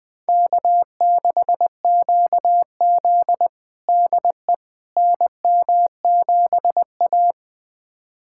Morse code, heard as a keyed tone, about 20 wpm, 700 Hz. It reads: K6QZ DE NM7A
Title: USA Callsign DE Callsign - 4 Characters - 20wpm